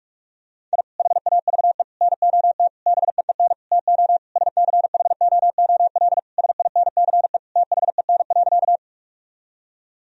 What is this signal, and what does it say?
Morse code, heard as a keyed tone, about 46 wpm, 700 Hz. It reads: I HAVE NOT BEEN TO SCHOOL SINCE THEN.
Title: Sentences from Top 1000 Words - Review Part 1 - 45wpm